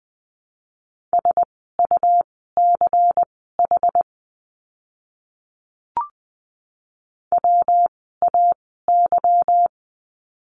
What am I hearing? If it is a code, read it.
Morse code, heard as a keyed tone, about 20 wpm, spaced as for 15 wpm, 700 Hz. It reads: SUCH WAY
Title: 192 Lesson 31f - Letter v - Words 20wpm